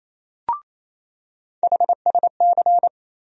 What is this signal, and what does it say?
Morse code, heard as a keyed tone, about 27 wpm, 700 Hz. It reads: HSC
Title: USA Call Sign Postfixes - 3 Letters 28wpm